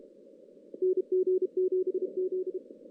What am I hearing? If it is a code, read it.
Morse code, heard as a keyed tone, about 34 wpm, 410 Hz. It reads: NG7Z
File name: 2016 JAN CW NAQP / 2016 Jan naqp k5ot - 0-6